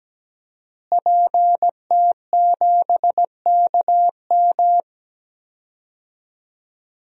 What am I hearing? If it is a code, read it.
Morse code, heard as a keyed tone, about 17 wpm, 700 Hz. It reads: PT7KM
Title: DXCC-100 Call Signs - 5 Characters - Repeated 2x - 17wpm